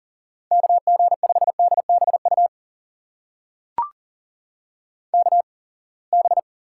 Morse code, heard as a keyed tone, about 40 wpm, spaced as for 28 wpm, 700 Hz. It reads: KG5DBU KB
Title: USA Call Signs - 6 Characters - 8x Character Spacing - 40wpm